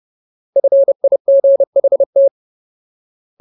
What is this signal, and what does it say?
Morse code, heard as a keyed tone, about 30 wpm, 550 Hz. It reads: FIGHT
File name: Ultimate Word List / 30WPM Ultimate